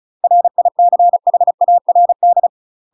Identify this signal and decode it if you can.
Morse code, heard as a keyed tone, about 34 wpm, 700 Hz. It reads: RICHARD
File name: UK MASTER QSOs TRAINING / 35WPM / UK MASTER QSOs 0005